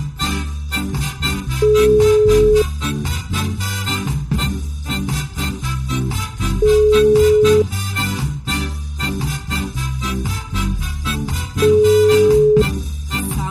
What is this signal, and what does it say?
Ringback tone (Europe, 425 Hz), with noise.